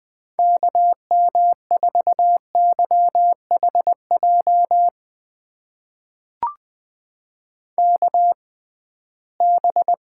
Morse code, heard as a keyed tone, about 20 wpm, spaced as for 15 wpm, 700 Hz. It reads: KM4YHJ KB
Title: USA Call Signs - 6 Characters - 6x Character Spacing - 20wpm